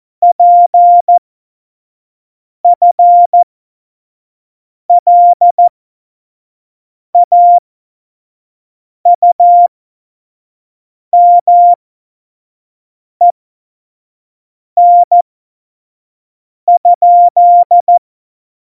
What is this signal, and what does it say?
Morse code, heard as a keyed tone, about 14 wpm, spaced as for 5 wpm, 700 Hz. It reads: PFLAUMEN?
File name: Morsekurs / Witze / Witz 12. November 2019, 25 F70